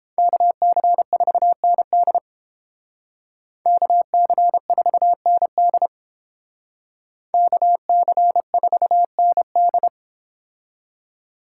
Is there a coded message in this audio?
Yes: Morse code from 0.2 to 9.9 s, heard as a keyed tone, about 30 wpm, 700 Hz. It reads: KC4ND KC4ND KC4ND